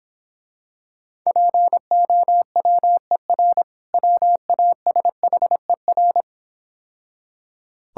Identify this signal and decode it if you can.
Morse code, heard as a keyed tone, about 26 wpm, 690 Hz. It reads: POWER WASHER